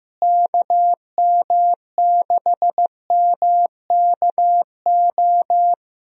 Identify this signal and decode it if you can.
Morse code, heard as a keyed tone, about 16 wpm, 700 Hz. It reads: KM6MKO
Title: USA Call Signs - 6 Characters - Repeated 3x - 15wpm